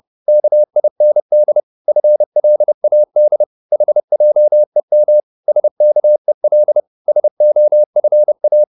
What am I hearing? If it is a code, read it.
Morse code, heard as a keyed tone, about 30 wpm, 600 Hz. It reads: KIND FLAD HJEM SKEL SOFA
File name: Korte ord DK 30wpm